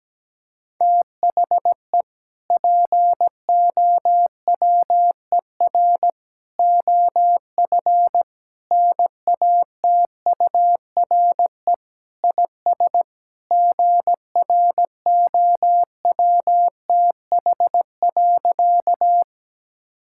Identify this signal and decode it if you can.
Morse code, heard as a keyed tone, about 17 wpm, 700 Hz. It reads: THE POWER OF NATURE IS GROWTH.
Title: Sentences from Top 1000 Words - Review Part 5 - 17wpm